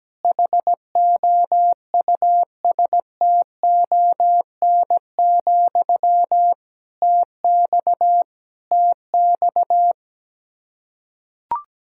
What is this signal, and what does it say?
Morse code, heard as a keyed tone, about 17 wpm, 700 Hz. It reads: HOUSTON, TX TX
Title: K1USN SST QSO Practice 17wpm